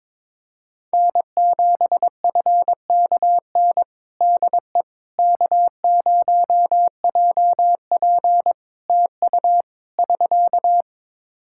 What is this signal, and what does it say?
Morse code, heard as a keyed tone, about 22 wpm, 700 Hz. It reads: N7FKN DE K0JP TU <SK>
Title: Practice QSO - Exchange 4 - Ending - 22wpm